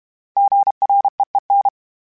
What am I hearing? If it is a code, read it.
Morse code, heard as a keyed tone, about 32 wpm, 800 Hz. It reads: GREEN
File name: fromSMs / FromSMs mp3s / Cities@30